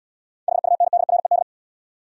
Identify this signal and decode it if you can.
Morse code, heard as a keyed tone, about 72 wpm, 700 Hz. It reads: HUNDRED